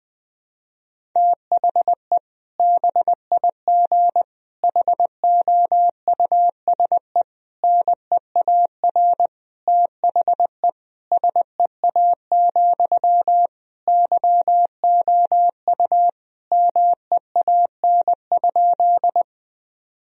Morse code, heard as a keyed tone, about 20 wpm, 700 Hz. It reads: THE BIG HOUSE NEAR THE SEA, YOU MEAN?
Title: Sentences from Top 1000 Words - Review Part 1 - 20wpm